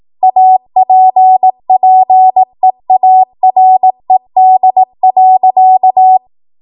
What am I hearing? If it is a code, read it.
Morse code, heard as a keyed tone, about 18 wpm, 750 Hz. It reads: APPEARED.
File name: Archive / 18 WPM / 18WPM